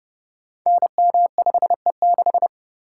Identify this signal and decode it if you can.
Morse code, heard as a keyed tone, about 30 wpm, 700 Hz. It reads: NM5E6
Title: Code Groups - 5 Letter:Numbers - 30wpm